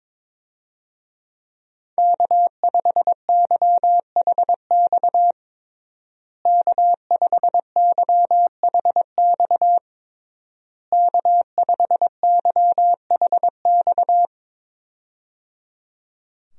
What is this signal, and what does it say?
Morse code, heard as a keyed tone, about 22 wpm, 700 Hz. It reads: K5YHX K5YHX K5YHX
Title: USA Call Signs - 5 Characters - Repeated 3x - 22wpm